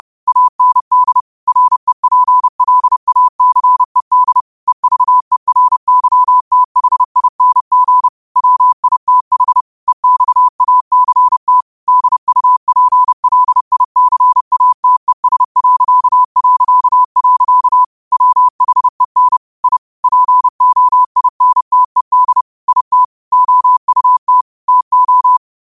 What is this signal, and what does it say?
Morse code, heard as a keyed tone, about 30 wpm, 1000 Hz. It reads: AND REPLACED EVERYTHING WITH EXACT DUPLICATES... WHEN I POINTED IT OUT TO